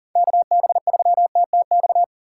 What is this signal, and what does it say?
Morse code, heard as a keyed tone, about 40 wpm, 700 Hz. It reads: KB3TTX
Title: USA Call Signs - 6 Characters - 2x Character Spacing - 40wpm